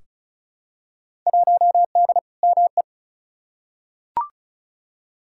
Morse code, heard as a keyed tone, about 36 wpm, 700 Hz. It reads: 1D ME
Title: Field Day Exchanges - 35wpm